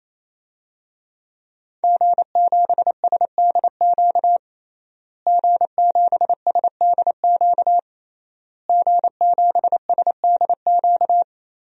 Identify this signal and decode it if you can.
Morse code, heard as a keyed tone, about 28 wpm, 700 Hz. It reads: G7SDQ G7SDQ G7SDQ
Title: DXCC-100 Call Signs - 5 Characters - Repeated 3x - 28wpm